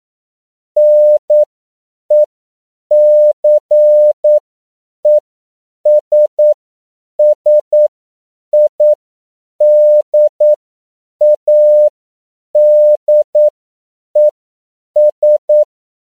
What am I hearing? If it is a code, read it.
Morse code, heard as a keyed tone, about 9 wpm, spaced as for 7 wpm, 600 Hz. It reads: NECESSIDADES
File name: CW / 10PPM